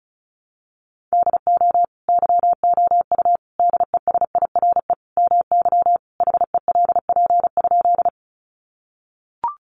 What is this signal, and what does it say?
Morse code, heard as a keyed tone, about 34 wpm, 700 Hz. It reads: DO YOU DESIRE MY HELP?
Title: Sentences from Top 1100 Words 35wpm